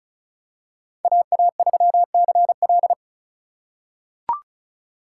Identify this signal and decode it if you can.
Morse code, heard as a keyed tone, about 36 wpm, 700 Hz. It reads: AA3CL
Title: USA Call Signs - 6x Character Spacing - 35wpm